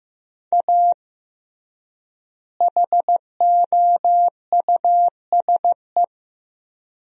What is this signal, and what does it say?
Morse code, heard as a keyed tone, about 16 wpm, 700 Hz. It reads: A HOUSE
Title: 3x Word-Spacing - Sentences from Top 400 Words - 15wpm